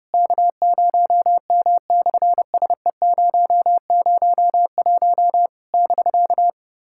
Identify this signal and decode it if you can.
Morse code, heard as a keyed tone, about 30 wpm, 700 Hz. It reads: K0M/SE001 <BK>
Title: SOTA QSO Practice 30wpm